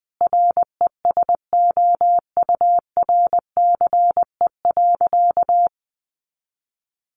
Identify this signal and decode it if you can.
Morse code, heard as a keyed tone, about 19 wpm, 700 Hz. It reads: RESOURCE.
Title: Sentences from Top 1300 Words 20wpm